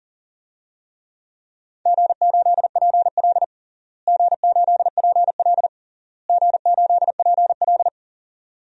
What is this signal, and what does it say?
Morse code, heard as a keyed tone, about 40 wpm, 700 Hz. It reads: G8PL G8PL G8PL
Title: DXCC-100 Call Signs - 4 Characters - Repeated 3x - 40wpm